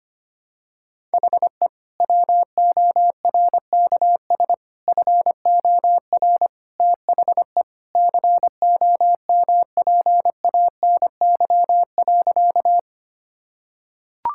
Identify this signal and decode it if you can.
Morse code, heard as a keyed tone, about 25 wpm, 700 Hz. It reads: HE WORKS FOR THE COMPANY.
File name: practice / 25wpm / Sentences with 2-Word Phrases - 25wpm